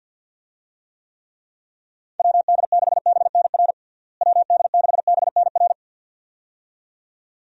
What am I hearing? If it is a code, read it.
Morse code, heard as a keyed tone, about 51 wpm, 700 Hz. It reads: WD6BNR WD6BNR
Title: USA Call Signs - 6 Characters - Repeated 2x - 50wpm